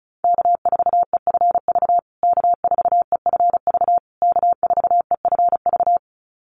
Morse code, heard as a keyed tone, about 34 wpm, 700 Hz. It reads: K4EFV K4EFV K4EFV
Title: USA Call Signs Repeated 3x 35wpm